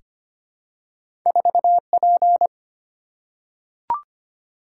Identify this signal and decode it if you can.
Morse code, heard as a keyed tone, about 25 wpm, 700 Hz. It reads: 4P
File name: Two Letter-Number Combinations - 25wpm